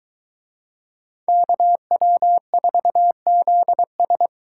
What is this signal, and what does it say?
Morse code, heard as a keyed tone, about 22 wpm, 700 Hz. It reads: KW4ZS